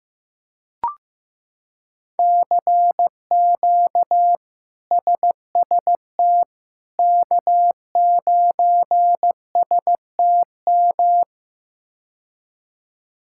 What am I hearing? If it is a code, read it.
Morse code, heard as a keyed tone, about 15 wpm, 700 Hz. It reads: CQ SST K9STM
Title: K1USN SST QSO Practice - 15wpm